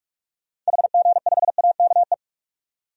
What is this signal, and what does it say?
Morse code, heard as a keyed tone, about 46 wpm, 700 Hz. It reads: HG5AKE
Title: DXCC-100 Call Signs - 6 Characters - 45wpm